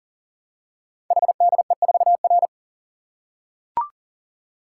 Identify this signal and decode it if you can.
Morse code, heard as a keyed tone, about 43 wpm, 700 Hz. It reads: HDE4R
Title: Code Groups - 5 Letter:Numbers - 40wpm